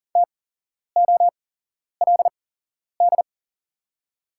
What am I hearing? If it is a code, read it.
Morse code, heard as a keyed tone, about 40 wpm, spaced as for 11 wpm, 700 Hz. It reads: TOLD
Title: 8x Character Spacing - Top 500 Words - 40wpm